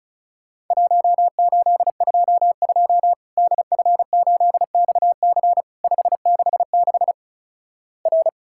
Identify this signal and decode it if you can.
Morse code, heard as a keyed tone, about 36 wpm, 700 Hz. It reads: 1822 DF8XC 566 R
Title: Worked All Europe DX Contest - QTC Exchange - 35wpm